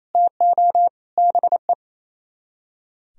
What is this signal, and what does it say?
Morse code, heard as a keyed tone, about 37 wpm, 700 Hz. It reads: TO BE